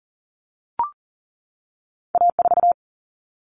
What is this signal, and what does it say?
Morse code, heard as a keyed tone, about 40 wpm, 700 Hz. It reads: A4
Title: Common Cut Numbers - 2-Digit 40wpm